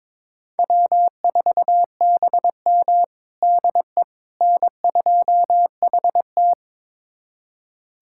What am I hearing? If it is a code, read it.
Morse code, heard as a keyed tone, about 22 wpm, 700 Hz. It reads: W4BM DE N2HT
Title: USA Callsign DE Callsign - 4 Characters - 22wpm